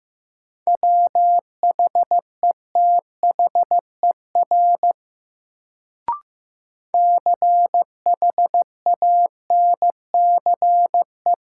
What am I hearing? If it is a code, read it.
Morse code, heard as a keyed tone, about 15 wpm, 700 Hz. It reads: WHETHER CHANCE
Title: Sets of 3 Words - New Word Drill - 15wpm